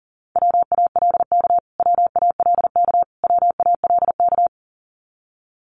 Morse code, heard as a keyed tone, about 37 wpm, 700 Hz. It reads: WALK WALK WALK
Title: Top 500 Words Repeated 40wpm